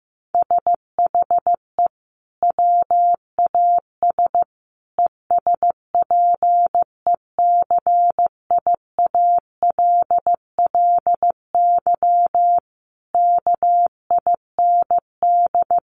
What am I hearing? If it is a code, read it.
Morse code, heard as a keyed tone, about 15 wpm, 700 Hz. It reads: SHE WAS ESPECIALLY KIND
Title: Sentences from Top 700 Words 15wpm